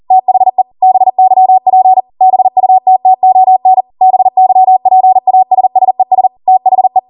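Morse code, heard as a keyed tone, about 40 wpm, 750 Hz. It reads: THE BYP BUTTON BYPASSES THE
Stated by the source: ARRL 40 WPM Morse code practice transmission.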